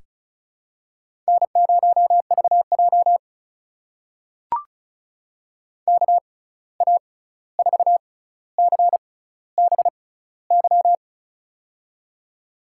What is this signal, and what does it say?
Morse code, heard as a keyed tone, about 35 wpm, spaced as for 16 wpm, 700 Hz. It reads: N0VJ KA4CBY
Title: USA Call Signs - 6x Character Spacing - 35wpm